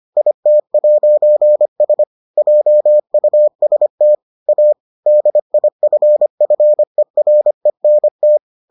Morse code, heard as a keyed tone, about 25 wpm, 600 Hz. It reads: IT'S JUST A DIFFERENT